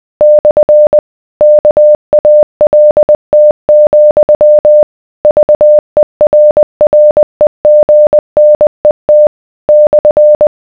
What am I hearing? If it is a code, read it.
Morse code, heard as a keyed tone, about 20 wpm, 600 Hz. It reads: / KALT, VERREGNET /